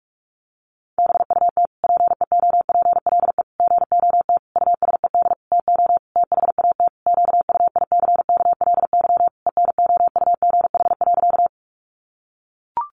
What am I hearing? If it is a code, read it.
Morse code, heard as a keyed tone, about 44 wpm, 700 Hz. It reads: BUT PEOPLE GOT USED TO THAT QUICKLY ENOUGH.
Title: Sentences from Top 700 Words 45wpm